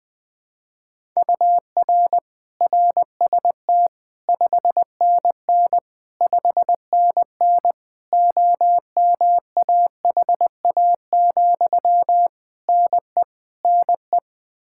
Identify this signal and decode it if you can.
Morse code, heard as a keyed tone, about 20 wpm, 700 Hz. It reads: UR RST 5NN 5NN OMAHA, NE NE
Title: Practice QSO - Exchange 1 - Beginning - 20wpm